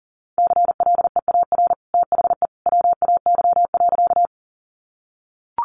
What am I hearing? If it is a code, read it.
Morse code, heard as a keyed tone, about 40 wpm, 700 Hz. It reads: CLEAR THE WAY.
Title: Sentences from Top 500 Words 40wpm